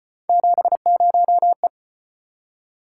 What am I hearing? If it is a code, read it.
Morse code, heard as a keyed tone, about 32 wpm, 700 Hz. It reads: 70E